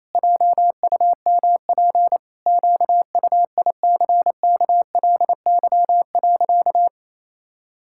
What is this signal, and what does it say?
Morse code, heard as a keyed tone, about 28 wpm, 700 Hz. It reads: JUMP QUICKLY.
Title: Pangram Challenge v2 - 28wpm